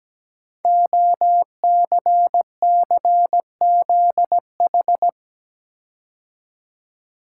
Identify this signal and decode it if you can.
Morse code, heard as a keyed tone, about 17 wpm, 700 Hz. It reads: OCCZH
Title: Code Groups - 5 Letters - 17wpm